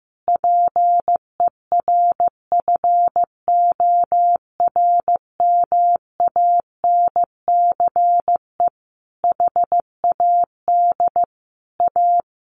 Morse code, heard as a keyed tone, about 15 wpm, 700 Hz. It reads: PERFORMANCE HAD A
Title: Sentences from Top 1400 Words 15wpm